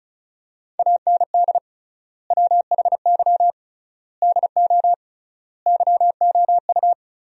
Morse code, heard as a keyed tone, about 36 wpm, 700 Hz. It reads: AND WHY DO YOU
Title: 3x Word-Spacing - Sentences from Top 400 Words - 35wpm